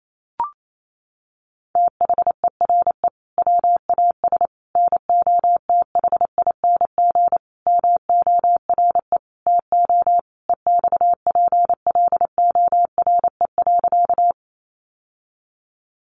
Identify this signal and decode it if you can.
Morse code, heard as a keyed tone, about 28 wpm, 700 Hz. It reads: THERE WAS NOTHING MORE TO EXPLORE.
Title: Sentences from Top 1100 Words 28wpm